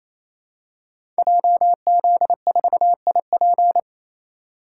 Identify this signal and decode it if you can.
Morse code, heard as a keyed tone, about 27 wpm, 700 Hz. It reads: JZ4IP
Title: Code Groups - 5 Letter:Numbers - 28wpm